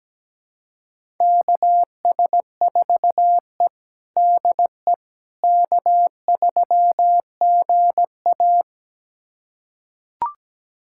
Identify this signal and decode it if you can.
Morse code, heard as a keyed tone, about 17 wpm, 700 Hz. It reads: KS4E DE K3GA
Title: USA Callsign DE Callsign - 4 Characters - 17wpm